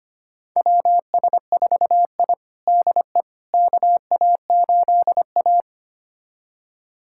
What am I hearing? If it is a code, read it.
Morse code, heard as a keyed tone, about 25 wpm, 700 Hz. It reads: WS4I DE KA8A
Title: USA Callsign DE Callsign - 4 Characters - 25wpm